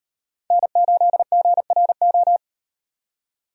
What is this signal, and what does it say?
Morse code, heard as a keyed tone, about 38 wpm, 700 Hz. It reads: N8GRO